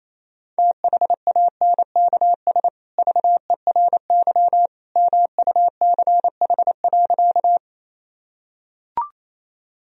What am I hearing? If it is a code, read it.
Morse code, heard as a keyed tone, about 28 wpm, 700 Hz. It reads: THANKS VERY MUCH.
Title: Sentences from Top 1000 Words - Review Part 3 - 28wpm